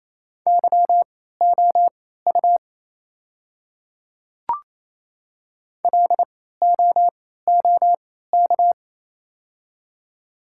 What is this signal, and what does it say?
Morse code, heard as a keyed tone, about 28 wpm, spaced as for 16 wpm, 700 Hz. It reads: YOU LOOK
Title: 3x Character Spacing - Top 500 Words - 28wpm